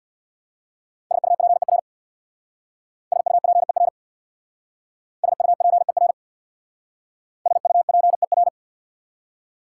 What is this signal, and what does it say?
Morse code, heard as a keyed tone, about 66 wpm, 700 Hz. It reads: SUPER SUPER SUPER SUPER